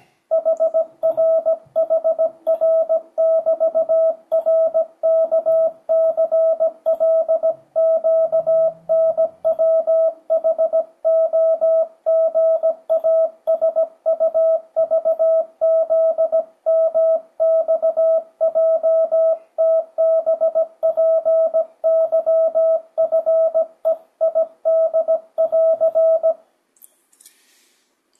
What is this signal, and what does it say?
Morse code, heard as a keyed tone, about 17 wpm, 650 Hz, with radio noise. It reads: HRHR<BT>RKCLQNWHOGASUVZMXJTBPYFEID<AR>
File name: 2025cw / speedup / 17wpm